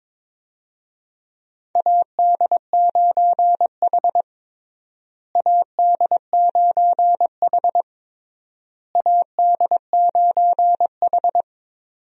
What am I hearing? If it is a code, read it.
Morse code, heard as a keyed tone, about 23 wpm, 700 Hz. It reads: AD9H AD9H AD9H
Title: USA Call Signs - 4 Characters - Repeated 3x - 22wpm